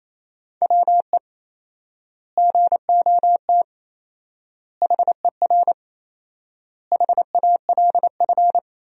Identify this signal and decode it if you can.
Morse code, heard as a keyed tone, about 28 wpm, 700 Hz. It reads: WE GOT HER HALF
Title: 4x Word-Spacing - Sentences from Top 400 Words - 28wpm